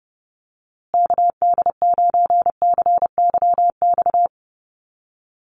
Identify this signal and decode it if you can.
Morse code, heard as a keyed tone, about 30 wpm, 700 Hz. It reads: KD9CYX
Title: USA Call Signs Repeated 3x 30wpm